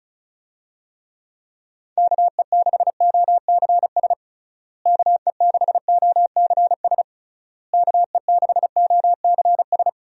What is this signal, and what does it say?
Morse code, heard as a keyed tone, about 34 wpm, 700 Hz. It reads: KE6OCS KE6OCS KE6OCS
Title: USA Call Signs - 6 Characters - Repeated 3x - 35wpm